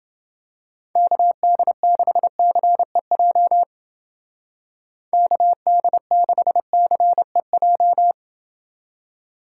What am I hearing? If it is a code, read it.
Morse code, heard as a keyed tone, about 29 wpm, 700 Hz. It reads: KD6CEJ KD6CEJ